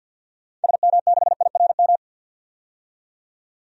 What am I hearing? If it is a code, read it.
Morse code, heard as a keyed tone, about 53 wpm, 700 Hz. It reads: SM6IRM
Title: DXCC-100 Call Signs - 6 Characters - 50wpm